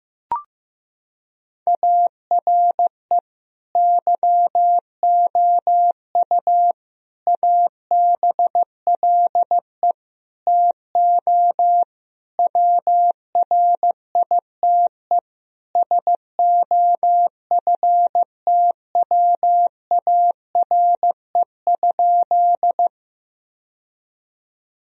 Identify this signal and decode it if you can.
Morse code, heard as a keyed tone, about 15 wpm, 700 Hz. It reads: ARE YOU ABLE TO WRITE SOFTWARE?
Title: Sentences from Top 1000 Words - Review Part 3 - 15wpm